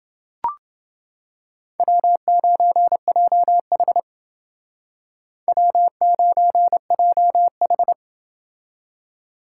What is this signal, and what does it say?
Morse code, heard as a keyed tone, about 28 wpm, 700 Hz. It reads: W9JH W9JH